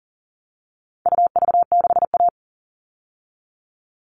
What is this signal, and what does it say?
Morse code, heard as a keyed tone, about 40 wpm, 700 Hz. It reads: UV6A
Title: All Cut Numbers - 4-Digit 40wpm